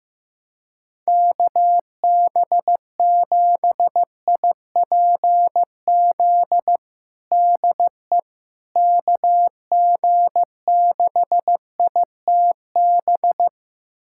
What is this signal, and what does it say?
Morse code, heard as a keyed tone, about 15 wpm, 700 Hz. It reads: KB7IPZ DE KG6ITB
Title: USA Callsign DE Callsign 15wpm